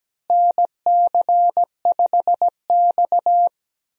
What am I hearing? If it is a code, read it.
Morse code, heard as a keyed tone, about 17 wpm, 700 Hz. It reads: NC5X
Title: USA Call Signs - 4 Characters - Repeated 3x - 17wpm